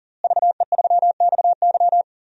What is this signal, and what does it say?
Morse code, heard as a keyed tone, about 40 wpm, 700 Hz. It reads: VE3XY
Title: DXCC-100 Call Signs - 5 Characters - Repeated 3x - 40wpm